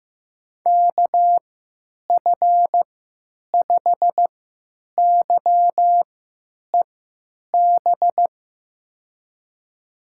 Morse code, heard as a keyed tone, about 15 wpm, spaced as for 9 wpm, 700 Hz. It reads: KF5YEB
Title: USA Call Signs - 6 Characters - 3x Character Spacing - 15wpm